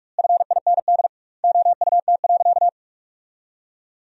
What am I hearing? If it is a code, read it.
Morse code, heard as a keyed tone, about 46 wpm, 700 Hz. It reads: FIND OUT.
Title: 2x Word-Spacing - Sentences from Top 200 Words - 45wpm